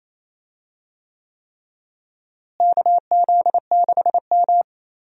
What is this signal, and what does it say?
Morse code, heard as a keyed tone, about 28 wpm, 700 Hz. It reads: KZ6M
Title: USA Call Signs - 4 Characters - Repeated 3x - 28wpm